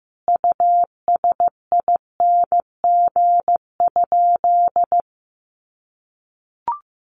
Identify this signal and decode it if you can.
Morse code, heard as a keyed tone, about 15 wpm, 700 Hz. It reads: USING?
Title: Common QSO Phrases 15wpm